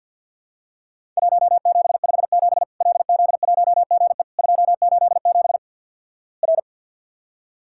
Morse code, heard as a keyed tone, about 50 wpm, 700 Hz. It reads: 1757 LZ1GE 287 R
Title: Worked All Europe DX Contest - QTC Exchange - 50wpm